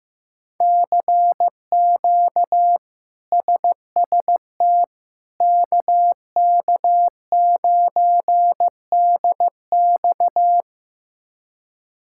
Morse code, heard as a keyed tone, about 15 wpm, 700 Hz. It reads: CQ SST KK9DX
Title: K1USN SST QSO Practice - 15wpm